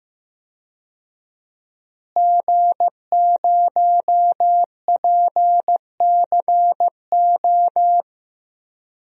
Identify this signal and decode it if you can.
Morse code, heard as a keyed tone, about 16 wpm, 700 Hz. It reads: G0PCO
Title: DXCC-100 Call Signs - 5 Characters - Repeated 3x - 15wpm